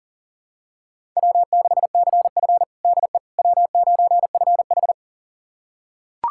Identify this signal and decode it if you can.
Morse code, heard as a keyed tone, about 37 wpm, 700 Hz. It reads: W6CF DE W9FH
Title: USA Callsign DE Callsign - 4 Characters - 40wpm